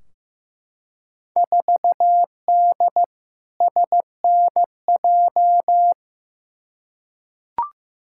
Morse code, heard as a keyed tone, about 15 wpm, 700 Hz. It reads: 4D SNJ
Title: Field Day Exchanges Encore - 15wpm